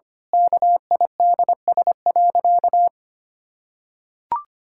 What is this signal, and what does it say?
Morse code, heard as a keyed tone, about 25 wpm, 700 Hz. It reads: KIDS.
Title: Pangram Challenge v2 - 25wpm